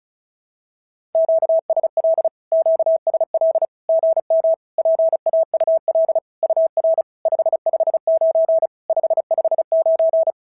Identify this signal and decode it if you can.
Morse code, heard as a keyed tone, about 34 wpm, 650 Hz. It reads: QSL QSL GM PAUL UR 559 559
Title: SOTA QSO Practice 35wpm